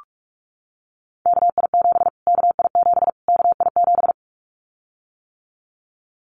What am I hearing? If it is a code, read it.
Morse code, heard as a keyed tone, about 44 wpm, 700 Hz. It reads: KI7 KI7 KI7
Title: Callsign Prefixes for Top 25 DXCC - Repeated 3x - 45wpm